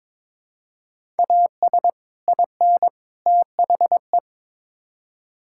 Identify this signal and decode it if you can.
Morse code, heard as a keyed tone, about 21 wpm, 700 Hz. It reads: AS IN THE
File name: practice / 22wpm / Build A Phrase - 5-Word Phrases - 22wpm